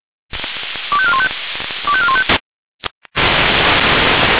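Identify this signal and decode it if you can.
Five-tone selective call (ZVEI), heard as quick runs of five tones, tones of about 70 ms. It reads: 25415 25415
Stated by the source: Five-tone ZVEI selective call received on an SDR.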